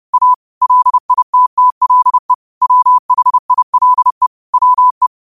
Morse code, heard as a keyed tone, about 30 wpm, 1000 Hz. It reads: A LITTLE WHILE WE
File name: qotdfiles / 30wpm